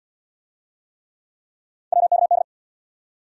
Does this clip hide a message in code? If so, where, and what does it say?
Yes: Morse code from 1.9 to 2.4 s, heard as a keyed tone, about 69 wpm, 700 Hz. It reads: WKD